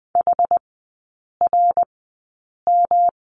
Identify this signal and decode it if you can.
Morse code, heard as a keyed tone, about 20 wpm, spaced as for 8 wpm, 700 Hz. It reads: HRM
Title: 114 Lesson 19e - Letter w - Three Characters 20wpm